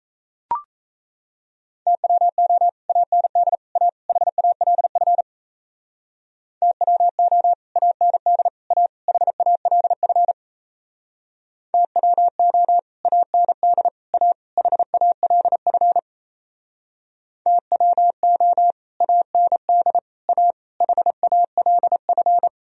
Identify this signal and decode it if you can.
Morse code, heard as a keyed tone, about 35 wpm, 700 Hz. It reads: TWO AND A HALF TWO AND A HALF TWO AND A HALF TWO AND A HALF
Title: Speed-Racing - Sets of 4 Words 28wpm